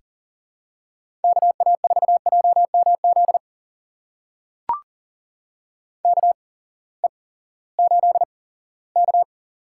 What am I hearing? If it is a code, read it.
Morse code, heard as a keyed tone, about 40 wpm, spaced as for 20 wpm, 700 Hz. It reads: KA4JMZ KE8K
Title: USA Call Signs - 6 Characters - 8x Character Spacing - 40wpm